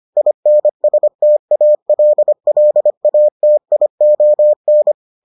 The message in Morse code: INSTALLATION

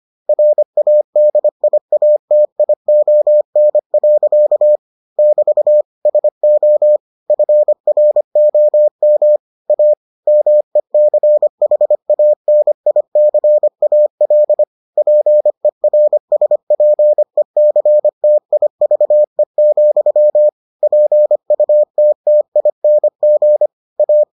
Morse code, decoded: RADIATION. <BT> SO FROM A MECHANICAL PERSPECTIVE, PUTTING A

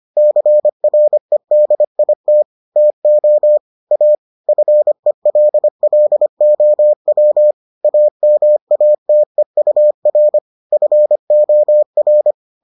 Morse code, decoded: CREDIT TO A FELLOW AMATEUR FOR

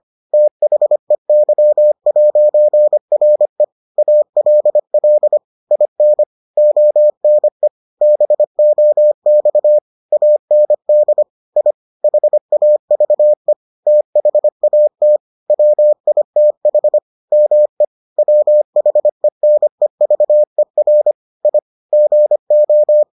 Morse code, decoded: THEY'RE ALL IN ONE BOX AND I HAVE THAT WITH ME WHENEVER I GO

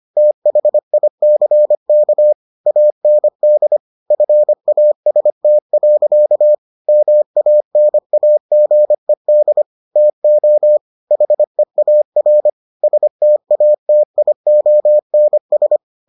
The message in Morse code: THICK AND FAST. MANAGED TO HEAR STATIONS

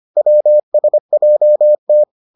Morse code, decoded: WSJT

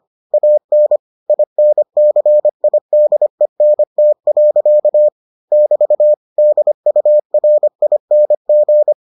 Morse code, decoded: AN INCIDENT. <BT> DURING

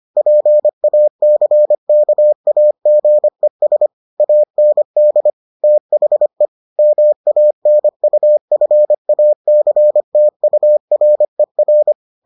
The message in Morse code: PACKAGES AND THE MANUFACTURER

